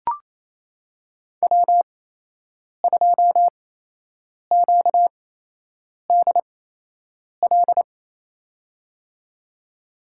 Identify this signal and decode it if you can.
Morse code, heard as a keyed tone, about 28 wpm, spaced as for 8 wpm, 700 Hz. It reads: W2QDL
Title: USA Call Signs - 8x Character Spacing - 28wpm